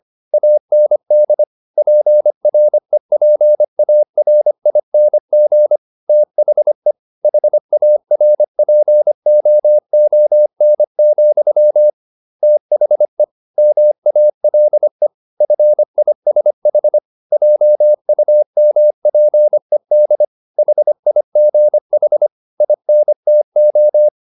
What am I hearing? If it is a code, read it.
Morse code, heard as a keyed tone, about 25 wpm, 600 Hz. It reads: AND PREPARING THE HARPOON, THE MALE FISH JUMPED HIGH INTO
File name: the old man and the sea part 1 25wpm